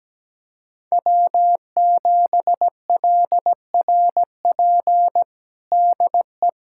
Morse code, heard as a keyed tone, about 17 wpm, 700 Hz. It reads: W7LRP DE
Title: USA Callsign DE Callsign 17wpm